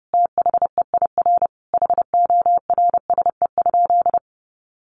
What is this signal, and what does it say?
Morse code, heard as a keyed tone, about 30 wpm, 700 Hz. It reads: THEIR HORSE?
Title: Sentences from Top 1400 Words 30wpm